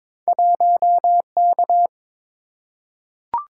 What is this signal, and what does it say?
Morse code, heard as a keyed tone, about 23 wpm, 700 Hz. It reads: 1K
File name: practice / 22wpm / Two Letter-Number Combinations - 22wpm